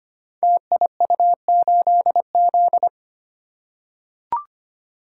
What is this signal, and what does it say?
Morse code, heard as a keyed tone, about 25 wpm, 700 Hz. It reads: TIU8Z
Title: Code Groups - 5 Letter:Numbers - 25wpm